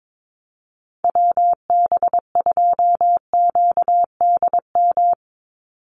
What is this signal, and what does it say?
Morse code, heard as a keyed tone, about 22 wpm, 700 Hz. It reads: WB2QDM
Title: USA Call Signs 22wpm